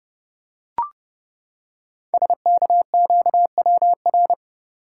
Morse code, heard as a keyed tone, about 29 wpm, 700 Hz. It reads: SKQWR
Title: Code Groups - 5 Letter:Numbers - 30wpm